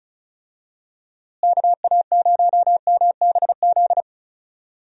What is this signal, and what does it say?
Morse code, heard as a keyed tone, about 34 wpm, 700 Hz. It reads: KA0MBZ
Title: USA Call Signs - 6 Characters - Repeated 2x - 35wpm